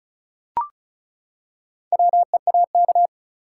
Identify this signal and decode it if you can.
Morse code, heard as a keyed tone, about 36 wpm, 700 Hz. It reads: WEAK
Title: Four Letter Words - 35wpm